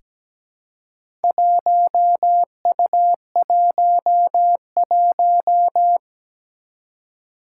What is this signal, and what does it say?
Morse code, heard as a keyed tone, about 17 wpm, 700 Hz. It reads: 1U11
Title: Four Challenging Letter-Number Combinations 17wpm